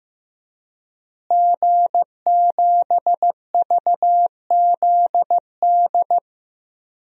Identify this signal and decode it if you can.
Morse code, heard as a keyed tone, about 15 wpm, 700 Hz. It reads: G7VZD
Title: DXCC-100 Call Signs - 5 Characters - Repeated 3x - 15wpm